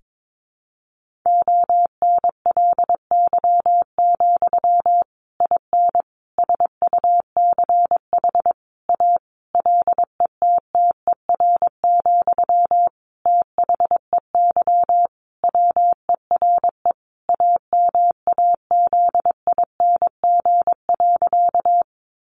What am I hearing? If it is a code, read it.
Morse code, heard as a keyed tone, about 22 wpm, 700 Hz. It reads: ONLY, IN SUCH A LETTER, THEY WERE AMAZING.
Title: Sentences from Top 1500 Words 22wpm